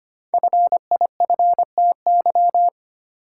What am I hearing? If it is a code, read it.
Morse code, heard as a keyed tone, about 25 wpm, 700 Hz. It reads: FIFTY
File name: 5 Letter Words - First 3 Letters Given - 25wpm